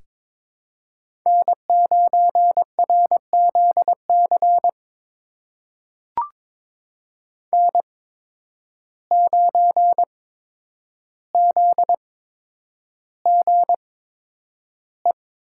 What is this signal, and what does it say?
Morse code, heard as a keyed tone, about 23 wpm, spaced as for 10 wpm, 700 Hz. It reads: N9RZC N9ZGE
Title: USA Call Signs - 8x Character Spacing - 22wpm